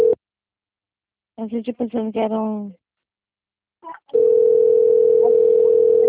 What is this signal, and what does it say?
Ringback tone (North America), 440 + 480 Hz, with speech.
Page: Marilia 2